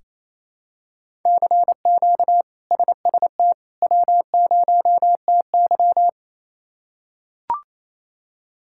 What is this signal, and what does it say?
Morse code, heard as a keyed tone, about 28 wpm, 700 Hz. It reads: CQ SST W0TY
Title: K1USN SST QSO Practice - 28wpm